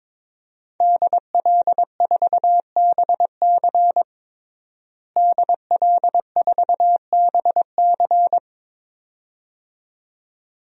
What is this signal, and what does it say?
Morse code, heard as a keyed tone, about 22 wpm, 700 Hz. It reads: DL4BC DL4BC
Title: DXCC-100 Call Signs - 5 Characters - Repeated 2x - 22wpm